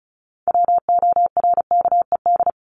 Morse code, heard as a keyed tone, about 34 wpm, 700 Hz. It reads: WORKED
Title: ED Suffix - Top 100 Words with Base Words 35wpm